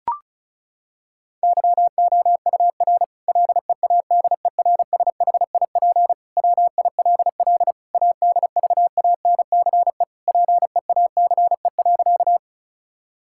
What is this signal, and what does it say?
Morse code, heard as a keyed tone, about 34 wpm, 700 Hz. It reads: YOUR LEADERSHIP WILL ADVANCE PEACE.
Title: Sentences from Top 1600 Words 35wpm